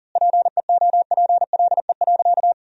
Morse code, heard as a keyed tone, about 40 wpm, 700 Hz. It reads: PEOPLE.
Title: 3x Word-Spacing - Sentences from Top 500 Words - 40wpm